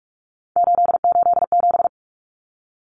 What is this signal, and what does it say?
Morse code, heard as a keyed tone, about 44 wpm, 700 Hz. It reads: 887
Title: Common Cut Numbers - 3-Digit 45wpm